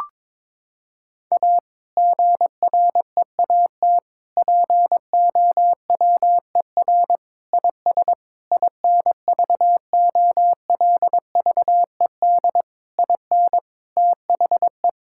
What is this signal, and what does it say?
Morse code, heard as a keyed tone, about 22 wpm, 700 Hz. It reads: A GREAT POWER IS INVOLVED IN THE
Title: Sentences from Top 1000 Words - Review Part 5 - 22wpm